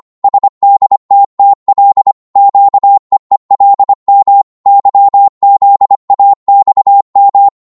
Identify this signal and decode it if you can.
Morse code, heard as a keyed tone, about 25 wpm, 800 Hz. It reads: SDTTL QEELM YZAXM
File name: Morse / ltr-25wpm